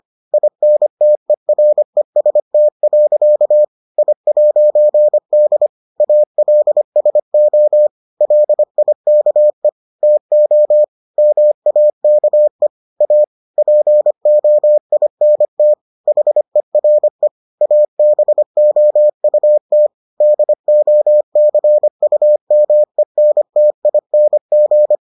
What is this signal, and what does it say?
Morse code, heard as a keyed tone, about 25 wpm, 600 Hz. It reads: INTEREST. I'D ALSO LIKE TO MAKE A POINT HERE ABOUT DOCUMENTING